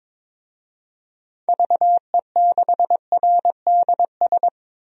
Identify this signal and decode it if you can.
Morse code, heard as a keyed tone, about 22 wpm, 700 Hz. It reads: VE6RDS
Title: DXCC-100 Call Signs - 6 Characters - 22wpm